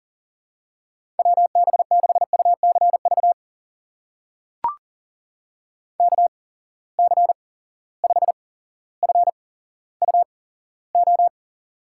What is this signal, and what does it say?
Morse code, heard as a keyed tone, about 40 wpm, spaced as for 17 wpm, 700 Hz. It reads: WB6UCV KC5FUO
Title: USA Call Signs - 8x Character Spacing - 40wpm